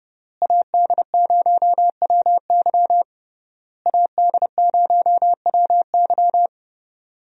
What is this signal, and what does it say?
Morse code, heard as a keyed tone, about 30 wpm, 700 Hz. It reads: AD0WY AD0WY
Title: USA Call Signs - 5 Characters - Repeated 3x - 30wpm